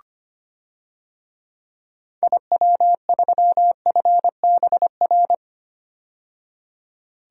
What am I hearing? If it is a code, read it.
Morse code, heard as a keyed tone, about 25 wpm, 700 Hz. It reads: IW3FBR
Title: DXCC-100 Call Signs - 6 Characters - 25wpm